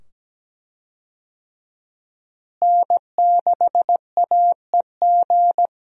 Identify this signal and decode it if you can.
Morse code, heard as a keyed tone, about 17 wpm, 700 Hz. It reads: N6AEG
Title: USA Call Signs - 5 Characters - Repeated 2x - 17wpm